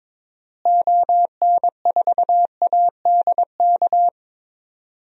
Morse code, heard as a keyed tone, about 21 wpm, 700 Hz. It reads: ON4ADK
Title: DXCC-100 Call Signs - 6 Characters - Repeated 3x - 22wpm